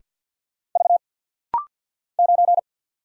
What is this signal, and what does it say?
Morse code, heard as a keyed tone, about 49 wpm, 700 Hz. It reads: V 9
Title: Single Letter-Number - Rapid-Fire 50wpm